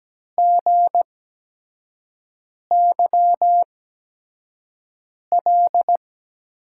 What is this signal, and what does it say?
Morse code, heard as a keyed tone, about 17 wpm, spaced as for 5 wpm, 700 Hz. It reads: GYL
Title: USA Call Signs - 6 Characters - 8x Character Spacing - 17wpm